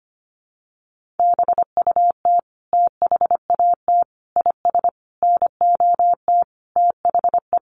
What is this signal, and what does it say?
Morse code, heard as a keyed tone, about 25 wpm, 700 Hz. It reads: BUT THAT IS NOT THE
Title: Sentences from Top 1500 Words 25wpm